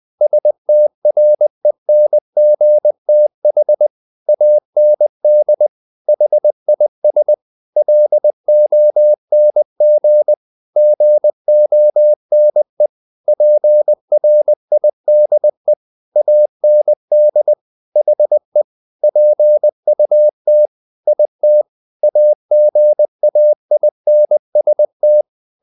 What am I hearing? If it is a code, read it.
Morse code, heard as a keyed tone, about 20 wpm, 600 Hz. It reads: STRENGTH AND HIS LONG GONE PRIDE AND HE PUT IT AGAINST
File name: the old man and the sea part 2 20wpm